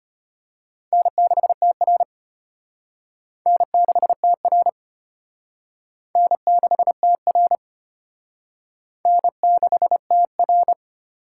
Morse code, heard as a keyed tone, about 33 wpm, 700 Hz. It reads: N6TR N6TR N6TR N6TR